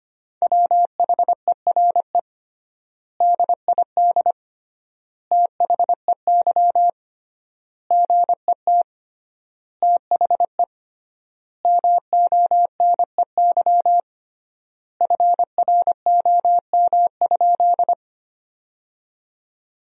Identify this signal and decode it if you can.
Morse code, heard as a keyed tone, about 25 wpm, 700 Hz. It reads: WHERE DID THEY GET THE MONEY FROM?
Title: 3x Word-Spacing - Sentences from Top 400 Words - 25wpm